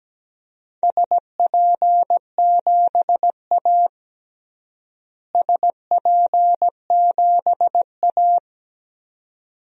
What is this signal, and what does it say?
Morse code, heard as a keyed tone, about 17 wpm, 700 Hz. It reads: SP7A SP7A
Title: DXCC-100 Call Signs - 4 Characters - Repeated 2x - 17wpm